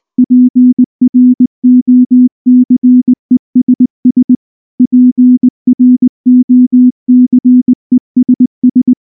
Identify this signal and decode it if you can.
Morse code, heard as a keyed tone, about 19 wpm, 250 Hz. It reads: PROCESS PROCESS
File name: morse / morse-training-20WPM-4